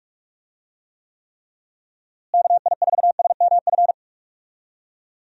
Morse code, heard as a keyed tone, about 45 wpm, 700 Hz. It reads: KI4SMF
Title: USA Call Signs - 6 Characters - Repeated 3x - 45wpm